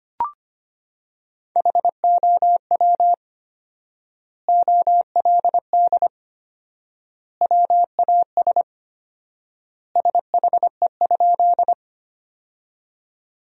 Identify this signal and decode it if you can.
Morse code, heard as a keyed tone, about 25 wpm, 700 Hz. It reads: HOW OLD WAS SHE?
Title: 4x Word-Spacing - Sentences from Top 200 Words - 25wpm